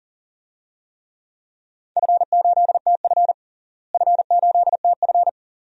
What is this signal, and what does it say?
Morse code, heard as a keyed tone, about 40 wpm, 700 Hz. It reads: F8TF F8TF
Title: DXCC-100 Call Signs - 4 Characters - Repeated 2x - 40wpm